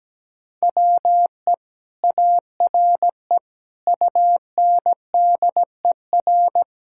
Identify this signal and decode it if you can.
Morse code, heard as a keyed tone, about 17 wpm, 700 Hz. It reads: WE ARE UNDER
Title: Sentences from Top 1000 Words - Review Part 6 - 17wpm